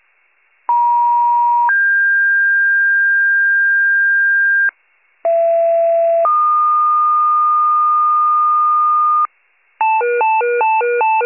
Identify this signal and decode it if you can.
Two-tone page, heard as a one-second tone, then a three-second tone, about 950 then 1600 Hz, about 670 then 1220 Hz, with noise. It is heard from 0.7 to 9.3 s.